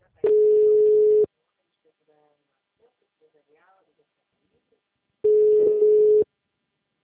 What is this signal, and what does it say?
Ringback tone (Europe, 425 Hz), with speech.